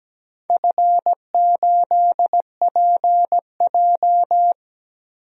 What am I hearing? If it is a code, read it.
Morse code, heard as a keyed tone, about 17 wpm, 700 Hz. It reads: F8PJ
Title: DXCC-100 Call Signs - 4 Characters - Repeated 2x - 17wpm